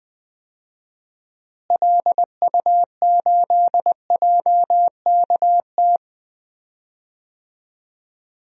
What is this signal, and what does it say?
Morse code, heard as a keyed tone, about 20 wpm, 700 Hz. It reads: LU8JKT
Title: DXCC-100 Call Signs - 6 Characters - Repeated 3x - 20wpm